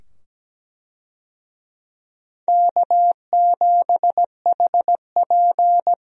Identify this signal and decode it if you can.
Morse code, heard as a keyed tone, about 17 wpm, 700 Hz. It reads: K7HP
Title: USA Call Signs - 4 Characters - 17wpm